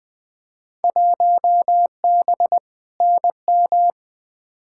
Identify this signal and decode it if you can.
Morse code, heard as a keyed tone, about 20 wpm, 700 Hz. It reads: 1B NM
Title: Field Day Exchanges Encore - 20wpm